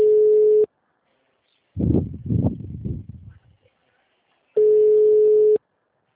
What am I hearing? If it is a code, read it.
Ringback tone (Europe, 425 Hz).
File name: out- -2004- - - .78